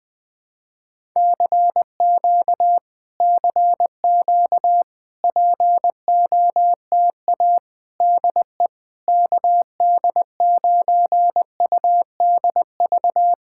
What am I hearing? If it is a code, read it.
Morse code, heard as a keyed tone, about 21 wpm, 700 Hz. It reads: CQ CQ POTA DE KD9UDV
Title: POTA QSO Practice - 20wpm